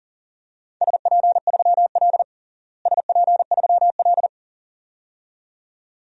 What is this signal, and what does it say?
Morse code, heard as a keyed tone, about 37 wpm, 700 Hz. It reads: SP3L SP3L
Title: DXCC-100 Call Signs - 4 Characters - Repeated 2x - 40wpm